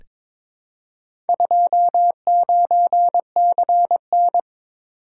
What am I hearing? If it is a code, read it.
Morse code, heard as a keyed tone, about 22 wpm, 700 Hz. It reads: 29CN
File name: practice / 22wpm / Four Letter-Number Combinations - 22wpm